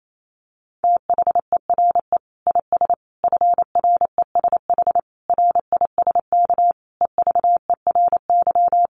Morse code, heard as a keyed tone, about 28 wpm, 700 Hz. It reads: THERE IS FRESH RISK EVERY
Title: Sentences from Top 1200 Words 28wpm